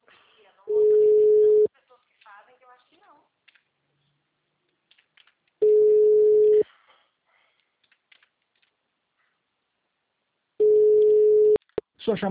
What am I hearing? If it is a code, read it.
Ringback tone (Europe, 425 Hz), with speech.